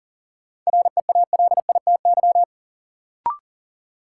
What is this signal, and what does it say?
Morse code, heard as a keyed tone, about 40 wpm, 700 Hz. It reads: REALITY
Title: Sentences from Top 1200 Words - New Word Drill Alt - 40wpm